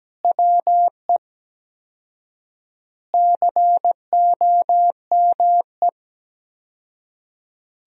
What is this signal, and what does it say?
Morse code, heard as a keyed tone, about 17 wpm, 700 Hz. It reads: WE COME
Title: 4x Word-Spacing - Sentences from Top 100 Words - 17wpm